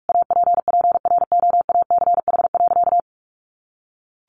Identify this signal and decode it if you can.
Morse code, heard as a keyed tone, about 44 wpm, 700 Hz. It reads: APPROACH.
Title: Sentences from Top 1400 Words 45wpm